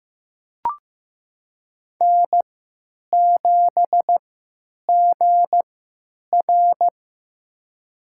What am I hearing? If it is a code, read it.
Morse code, heard as a keyed tone, about 15 wpm, spaced as for 9 wpm, 700 Hz. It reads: N7GR
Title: USA Call Signs - 4 Characters - 3x Character Spacing - 15wpm